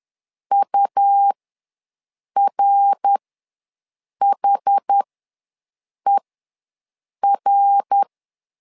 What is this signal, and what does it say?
Morse code, heard as a keyed tone, about 10 wpm, spaced as for 6 wpm, 800 Hz. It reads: URHER
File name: LEZIONI / 3 MWRO / 3 MWR ws 10wpm 2151